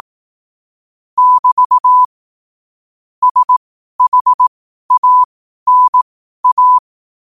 Morse code, heard as a keyed tone, about 18 wpm, spaced as for 12 wpm, 1000 Hz. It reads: <BT> SHANA
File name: qotdfiles / 12wpm